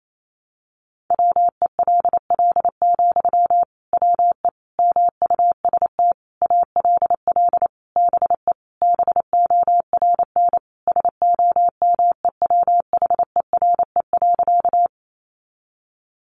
Morse code, heard as a keyed tone, about 28 wpm, 700 Hz. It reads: WELL, WE MUST ALL BE BORN SOMEWHERE.
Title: Sentences from Top 1400 Words 28wpm